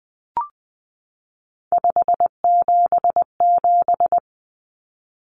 Morse code, heard as a keyed tone, about 20 wpm, 700 Hz. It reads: 577
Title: Numbers - 3-Digit 20wpm